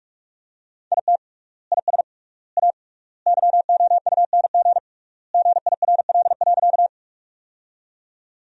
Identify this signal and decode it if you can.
Morse code, heard as a keyed tone, about 43 wpm, 700 Hz. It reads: IT IS A YOUNG GIRL.
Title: 3x Word-Spacing - Sentences from Top 400 Words - 45wpm